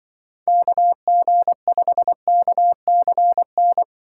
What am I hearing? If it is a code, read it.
Morse code, heard as a keyed tone, about 24 wpm, 700 Hz. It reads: KG5KCN
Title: Speed Racing - USA 2x3 Call Signs 20wpm